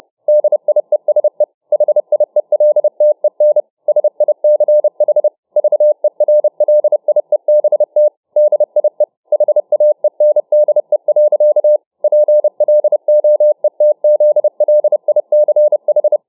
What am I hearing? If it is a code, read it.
Morse code, heard as a keyed tone, about 29 wpm, 600 Hz. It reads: DIESE HIELTEN SICH VERLIEBT DIE HAENDE. PLOETZLICH
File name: dk5ke / Morsetanz 30WpM